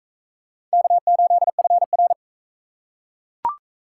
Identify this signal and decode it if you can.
Morse code, heard as a keyed tone, about 38 wpm, 700 Hz. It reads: K8FR